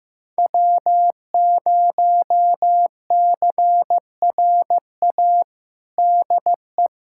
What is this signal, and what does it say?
Morse code, heard as a keyed tone, about 15 wpm, 700 Hz. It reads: W0CRA DE
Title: USA Callsign DE Callsign - 5 Characters - 15wpm